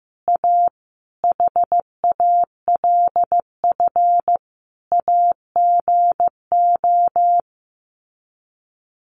Morse code, heard as a keyed tone, about 14 wpm, 700 Hz. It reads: A HALF AGO
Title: Sets of 5 Words Encore 15wpm